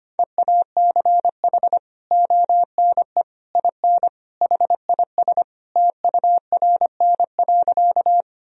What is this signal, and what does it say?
Morse code, heard as a keyed tone, about 25 wpm, 700 Hz. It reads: EACH ONE IN HIS TURN.
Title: Sentences from Top 1000 Words - Review Part 1 - 25wpm